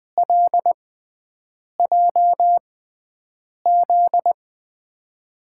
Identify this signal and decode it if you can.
Morse code, heard as a keyed tone, about 20 wpm, spaced as for 7 wpm, 700 Hz. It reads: LJZ
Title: USA Call Signs - 5 Characters - 6x Character Spacing - 20wpm